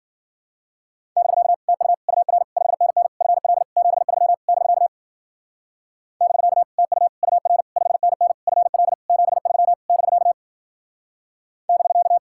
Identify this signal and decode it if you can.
Morse code, heard as a keyed tone, about 52 wpm, 700 Hz. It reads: <BK> TU UR 5NN FL 73 <BK> <BK> TU UR 5NN FL 73 <BK> <BK>